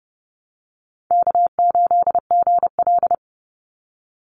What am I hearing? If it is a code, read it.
Morse code, heard as a keyed tone, about 30 wpm, 700 Hz. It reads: K8GL
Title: Call Signs - 30wpm